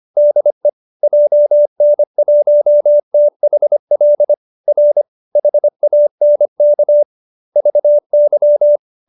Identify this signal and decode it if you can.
Morse code, heard as a keyed tone, about 25 wpm, 570 Hz. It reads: DE JN1THL R HANK VY
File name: QSO305 25